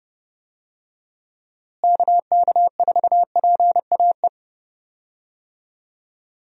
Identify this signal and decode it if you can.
Morse code, heard as a keyed tone, about 32 wpm, 700 Hz. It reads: KK4PAE
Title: USA Call Signs - 6 Characters - Repeated 3x - 30wpm